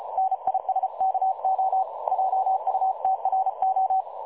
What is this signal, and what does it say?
Morse code, heard as a keyed tone, about 34 wpm, 760 Hz. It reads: NISKO JUTRO
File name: dlazaawansowanych / nagraniazpasma / websdr recording 3565.0kHz